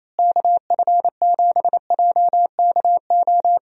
Morse code, heard as a keyed tone, about 28 wpm, 700 Hz. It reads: KF7JKO
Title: USA Call Signs - 6 Characters - Repeated 2x - 28wpm